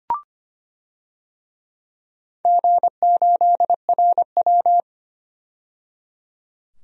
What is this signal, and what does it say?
Morse code, heard as a keyed tone, about 25 wpm, 700 Hz. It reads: G8RW
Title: DXCC-100 Call Signs - 4 Characters - 25wpm